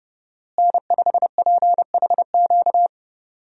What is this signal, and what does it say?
Morse code, heard as a keyed tone, about 30 wpm, 700 Hz. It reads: N5PHQ
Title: USA Call Signs - 5 Characters - Repeated 2x - 30wpm